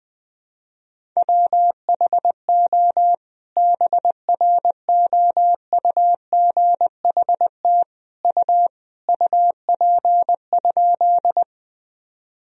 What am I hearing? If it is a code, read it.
Morse code, heard as a keyed tone, about 20 wpm, 700 Hz. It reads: WHO BROUGHT U UP?
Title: Sentences from Top 500 Words and Abbreviations 20wpm